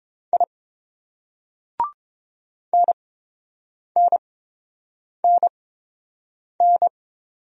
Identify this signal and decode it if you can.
Morse code, heard as a keyed tone, about 30 wpm, spaced as for 7 wpm, 700 Hz. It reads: I NNNN